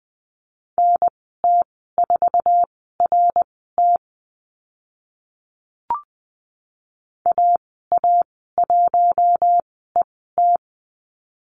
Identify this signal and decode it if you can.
Morse code, heard as a keyed tone, about 20 wpm, spaced as for 15 wpm, 700 Hz. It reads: NT4RT AA1ET
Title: 062 Lesson 11g - Letter h - Call Signs 20wpm